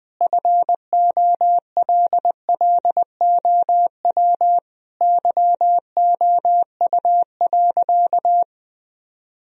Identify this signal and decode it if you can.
Morse code, heard as a keyed tone, about 19 wpm, 700 Hz. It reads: FOLLOW YOU.
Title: Sentences from Top 1000 Words - Review Part 1 - 20wpm